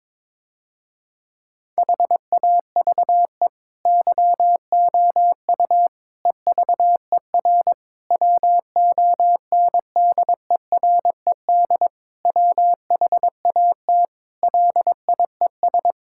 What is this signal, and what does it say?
Morse code, heard as a keyed tone, about 22 wpm, 700 Hz. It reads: HAVE YOU EVER WONDERED WHAT LIES
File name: Difficult Sentences with Hints - 22wpm